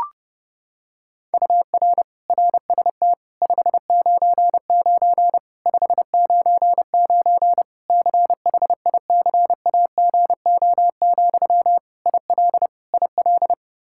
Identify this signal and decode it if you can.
Morse code, heard as a keyed tone, about 30 wpm, 700 Hz. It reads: UR RST 599 599 CHICAGO, IL IL
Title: Practice QSO - Exchange 1 - Beginning - 30wpm